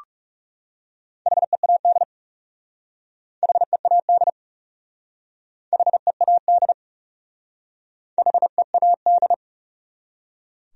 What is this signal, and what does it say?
Morse code, heard as a keyed tone, about 39 wpm, 700 Hz. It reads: HEAD HEAD HEAD HEAD